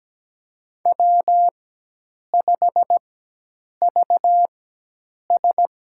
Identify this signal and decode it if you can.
Morse code, heard as a keyed tone, about 17 wpm, spaced as for 8 wpm, 700 Hz. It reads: W5VS
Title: USA Call Signs - 4x Character Spacing - 17wpm